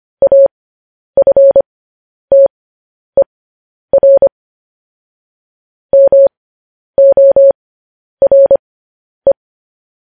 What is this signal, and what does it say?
Morse code, heard as a keyed tone, about 25 wpm, spaced as for 10 wpm, 550 Hz. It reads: AFTER MORE
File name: podcast / News headlines-1x-25x10